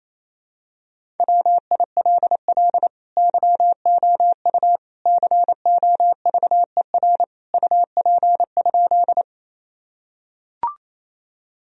Morse code, heard as a keyed tone, about 28 wpm, 700 Hz. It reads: WILL YOU COVER UP?
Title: Sentences from Top 1000 Words - Review Part 1 - 28wpm